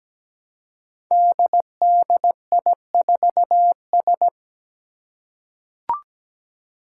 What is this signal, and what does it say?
Morse code, heard as a keyed tone, about 17 wpm, 700 Hz. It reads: DDI4S
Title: Five Challenging Letter-Number Combinations 17wpm